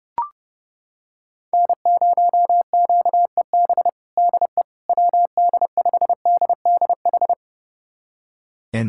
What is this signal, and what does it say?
Morse code, heard as a keyed tone, about 30 wpm, 700 Hz. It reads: N0QEB DE WD5DDH
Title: USA Callsign DE Callsign 30wpm